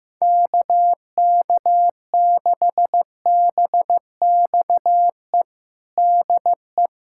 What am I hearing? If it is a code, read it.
Morse code, heard as a keyed tone, about 15 wpm, 700 Hz. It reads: KK6BXE DE
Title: USA Callsign DE Callsign - 6 Characters - 15wpm